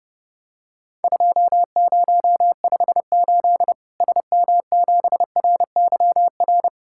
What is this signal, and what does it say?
Morse code, heard as a keyed tone, about 30 wpm, 700 Hz. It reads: 2058 SM7RYR
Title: Worked All Europe DX Contest - QTC Exchange - 30wpm